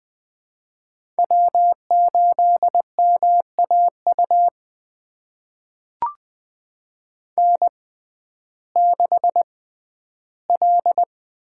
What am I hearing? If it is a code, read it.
Morse code, heard as a keyed tone, about 20 wpm, spaced as for 12 wpm, 700 Hz. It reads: W8MAU N6L
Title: USA Call Signs - 5 Characters - 6x Character Spacing - 20wpm